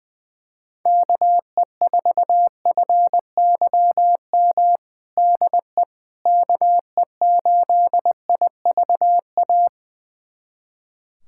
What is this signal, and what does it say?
Morse code, heard as a keyed tone, about 20 wpm, 700 Hz. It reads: KE4FYM DE KE8IVA
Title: USA Callsign DE Callsign - 6 Characters - 20wpm